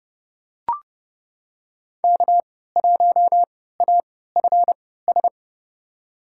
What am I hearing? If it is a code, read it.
Morse code, heard as a keyed tone, about 30 wpm, spaced as for 17 wpm, 700 Hz. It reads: K1AFS
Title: USA Call Signs - 5 Characters - 3x Character Spacing - 30wpm